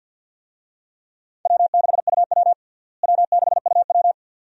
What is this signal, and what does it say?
Morse code, heard as a keyed tone, about 52 wpm, 700 Hz. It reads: W6UW W6UW
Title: USA Call Signs - 4 Characters - Repeated 2x - 50wpm